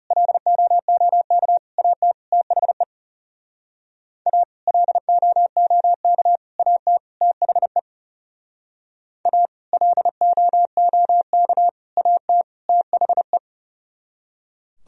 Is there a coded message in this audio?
Yes: Morse code from 0.1 to 13.4 s, heard as a keyed tone, about 40 wpm, 700 Hz. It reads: LOOK AT THE A LOOK AT THE A LOOK AT THE